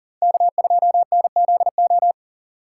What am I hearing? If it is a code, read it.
Morse code, heard as a keyed tone, about 52 wpm, 700 Hz. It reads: K2NZO